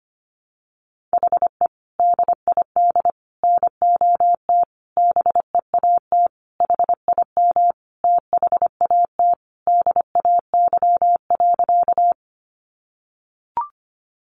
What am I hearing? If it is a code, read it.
Morse code, heard as a keyed tone, about 25 wpm, 700 Hz. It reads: HE DID NOT BEAT HIM THAT DAY.
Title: Sentences from Top 1300 Words 25wpm